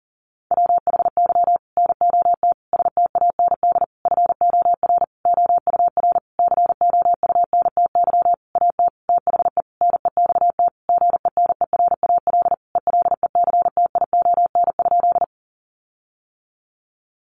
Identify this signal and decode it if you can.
Morse code, heard as a keyed tone, about 40 wpm, 700 Hz. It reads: WHY NOT STAND FOR OUR COUNTY AT THE NEXT GENERAL ELECTION?
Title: Sentences from Top 1500 Words 40wpm